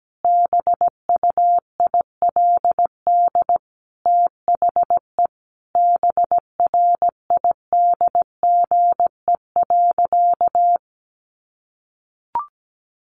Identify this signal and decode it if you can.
Morse code, heard as a keyed tone, about 17 wpm, 700 Hz. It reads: BUILD THE BRIDGE.
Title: Sentences from Top 1500 Words 17wpm